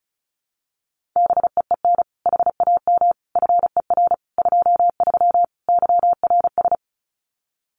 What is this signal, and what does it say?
Morse code, heard as a keyed tone, about 36 wpm, 700 Hz. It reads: BEEN HAM FER 23 YRS
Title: Common QSO Phrases Encore 35wpm